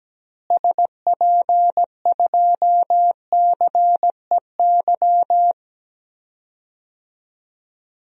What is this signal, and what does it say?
Morse code, heard as a keyed tone, about 17 wpm, 700 Hz. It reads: SP2CEY
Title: DXCC-100 Call Signs - 6 Characters - Repeated 3x - 17wpm